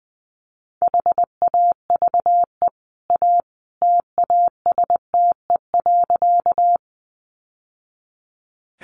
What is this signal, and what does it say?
Morse code, heard as a keyed tone, about 19 wpm, 700 Hz. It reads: HAVE A TASTE.
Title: Sentences from Top 1100 Words 20wpm